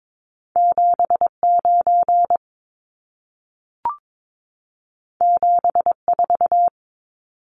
Morse code, heard as a keyed tone, about 22 wpm, 700 Hz. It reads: 79 74
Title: Numbers - 2-Digit 22wpm